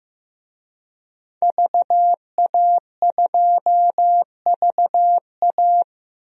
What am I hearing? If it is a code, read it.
Morse code, heard as a keyed tone, about 16 wpm, 700 Hz. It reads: VA2VA
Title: DXCC-100 Call Signs - 5 Characters - 15wpm